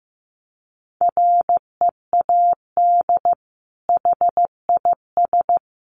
Morse code, heard as a keyed tone, about 15 wpm, 700 Hz. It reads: READ HIS
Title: Sentences from Top 1500 Words 15wpm